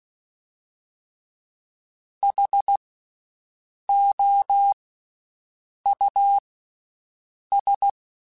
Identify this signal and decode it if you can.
Morse code, heard as a keyed tone, about 16 wpm, 800 Hz. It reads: H O U S
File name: Special INGs-mp3s / Special INGs-mp3s / Slow ing @15